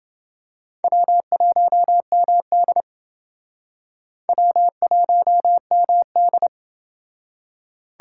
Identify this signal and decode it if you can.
Morse code, heard as a keyed tone, about 28 wpm, 700 Hz. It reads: W1MD W1MD